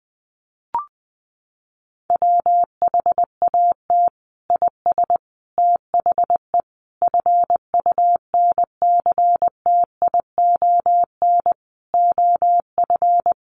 Morse code, heard as a keyed tone, about 20 wpm, 700 Hz. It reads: WHAT IS THE FUNCTION OF
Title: Sentences from Top 1100 Words 20wpm